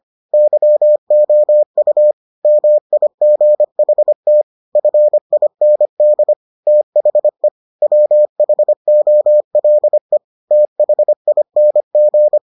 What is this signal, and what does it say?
Morse code, heard as a keyed tone, about 25 wpm, 600 Hz. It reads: YOU MIGHT FIND THE WHOLE THING